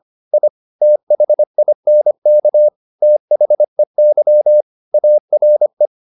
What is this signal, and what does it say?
Morse code, heard as a keyed tone, about 25 wpm, 600 Hz. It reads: I THINK THEY ARE
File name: the old man and the sea part 1 25wpm